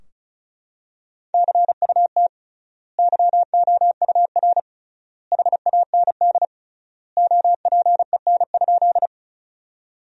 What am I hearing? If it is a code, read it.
Morse code, heard as a keyed tone, about 36 wpm, 700 Hz. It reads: CUT YOUR HAND OPEN?
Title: 3x Word-Spacing - Sentences from Top 300 Words - 35wpm